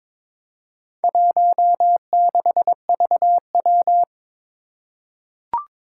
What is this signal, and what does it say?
Morse code, heard as a keyed tone, about 23 wpm, 700 Hz. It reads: 16VW
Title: Four Challenging Letter-Number Combinations 22wpm